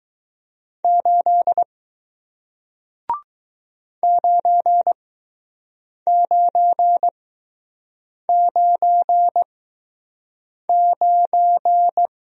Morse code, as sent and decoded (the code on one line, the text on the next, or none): ---.. / ----. ----. ----. ----.
8 9999